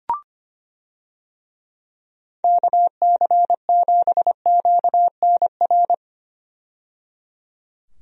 Morse code, heard as a keyed tone, about 25 wpm, 700 Hz. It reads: KC7QNR
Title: USA Call Signs - 6 Characters - 25wpm